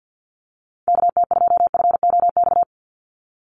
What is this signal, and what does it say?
Morse code, heard as a keyed tone, about 48 wpm, 700 Hz. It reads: KT2FOX
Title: USA Call Signs 50wpm